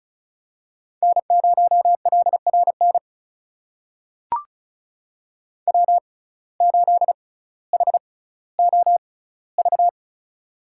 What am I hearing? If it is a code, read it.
Morse code, heard as a keyed tone, about 36 wpm, spaced as for 18 wpm, 700 Hz. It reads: N0LRN W8HOV
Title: USA Call Signs - 5 Characters - 6x Character Spacing - 35wpm